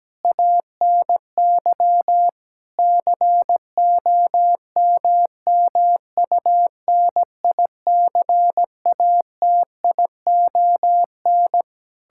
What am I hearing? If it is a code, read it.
Morse code, heard as a keyed tone, about 17 wpm, 700 Hz. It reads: ANY COMMUNICATION
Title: Sentences from Top 1000 Words - Review Part 6 - 17wpm